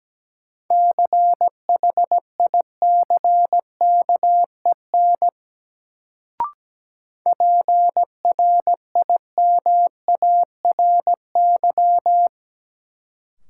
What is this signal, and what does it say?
Morse code, heard as a keyed tone, about 17 wpm, 700 Hz. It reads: CHICKEN PRIMARY
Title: Sentences from Top 1000 Words - Review Part 5 - New Word Drill - 17wpm